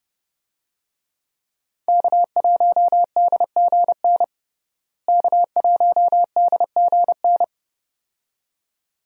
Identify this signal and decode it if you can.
Morse code, heard as a keyed tone, about 30 wpm, 700 Hz. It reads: K1DGN K1DGN
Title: USA Call Signs - 5 Characters - Repeated 2x - 30wpm